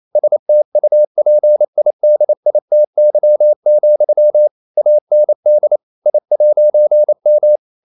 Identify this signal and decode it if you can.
Morse code, heard as a keyed tone, about 28 wpm, 600 Hz. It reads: STUPIDITY, AND I'M
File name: CW / Practice / EnglishQuotations0128